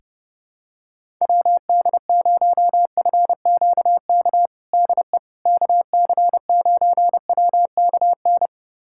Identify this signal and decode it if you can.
Morse code, heard as a keyed tone, about 30 wpm, 700 Hz. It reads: WD0FQK DE KC9WKN
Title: USA Callsign DE Callsign 30wpm